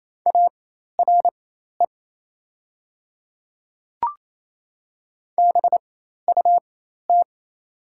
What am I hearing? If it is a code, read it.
Morse code, heard as a keyed tone, about 28 wpm, spaced as for 13 wpm, 700 Hz. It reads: ARE BUT
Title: 4x Character Spacing - Top 300 Words - 28wpm